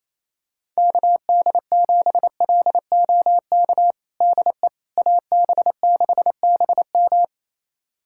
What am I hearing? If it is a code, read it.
Morse code, heard as a keyed tone, about 28 wpm, 700 Hz. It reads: KD7LOK DE AB6BM
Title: USA Callsign DE Callsign 28wpm